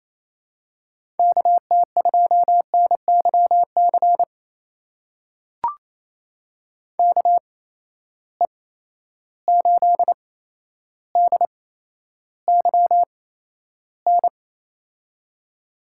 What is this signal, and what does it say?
Morse code, heard as a keyed tone, about 28 wpm, spaced as for 12 wpm, 700 Hz. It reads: KT2NYC KE8DYN
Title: USA Call Signs - 6 Characters - 8x Character Spacing - 28wpm